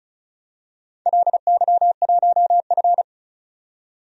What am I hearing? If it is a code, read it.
Morse code, heard as a keyed tone, about 36 wpm, 700 Hz. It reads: LY1F
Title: DXCC-100 Call Signs - 4 Characters - 35wpm